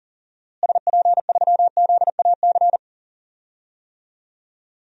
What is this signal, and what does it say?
Morse code, heard as a keyed tone, about 38 wpm, 700 Hz. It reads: SP3ZAC
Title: DXCC-100 Call Signs - 6 Characters - Repeated 3x - 40wpm